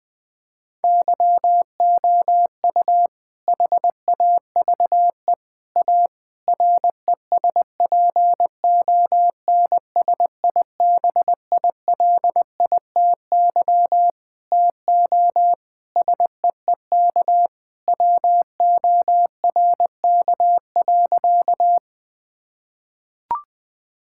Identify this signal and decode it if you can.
Morse code, heard as a keyed tone, about 20 wpm, 700 Hz. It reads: YOU HAVE A RESPONSIBILITY TO SEEK WORK.
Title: Sentences from Top 1600 Words 20wpm